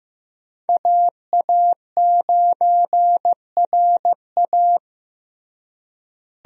Morse code, heard as a keyed tone, about 15 wpm, 700 Hz. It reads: AA9RA
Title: USA Call Signs - 5 Characters - Repeated 2x - 15wpm